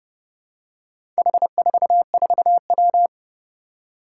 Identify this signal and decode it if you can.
Morse code, heard as a keyed tone, about 30 wpm, 700 Hz. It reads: H44W
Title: Four Challenging Letter-Number Combinations 30wpm